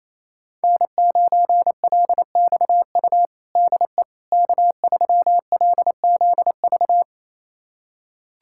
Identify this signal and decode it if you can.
Morse code, heard as a keyed tone, about 28 wpm, 700 Hz. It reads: N9LXU DE K3LZV
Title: USA Callsign DE Callsign - 5 Characters - 28wpm